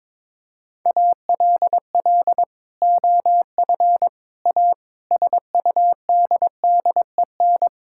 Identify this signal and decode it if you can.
Morse code, heard as a keyed tone, about 22 wpm, 700 Hz. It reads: ALL OF A SUDDEN
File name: practice / 22wpm / Sentences with 4-Word Phrases - 22wpm